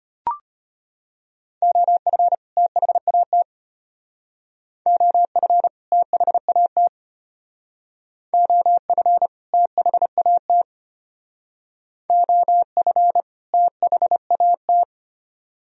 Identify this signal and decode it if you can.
Morse code, heard as a keyed tone, about 32 wpm, 700 Hz. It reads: OF THAT OF THAT OF THAT OF THAT